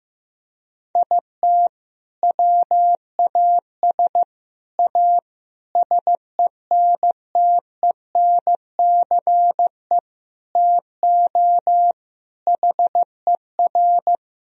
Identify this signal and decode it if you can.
Morse code, heard as a keyed tone, about 16 wpm, 700 Hz. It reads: IT WAS A SENTENCE TO HER
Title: Sentences from Top 1000 Words - Review Part 1 - 15wpm